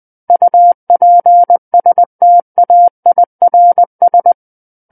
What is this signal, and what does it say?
Morse code, heard as a keyed tone, about 20 wpm, 700 Hz. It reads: UPSTAIRS
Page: SKCC Learning Center - Morse Code Practice Files